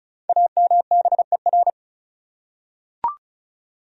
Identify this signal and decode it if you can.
Morse code, heard as a keyed tone, about 36 wpm, 700 Hz. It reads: AMBER
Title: USA Common Names - Max 5 Letters - 35wpm